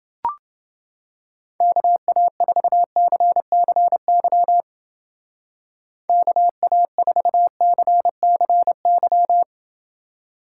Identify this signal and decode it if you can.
Morse code, heard as a keyed tone, about 28 wpm, 700 Hz. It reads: KA4CCY KA4CCY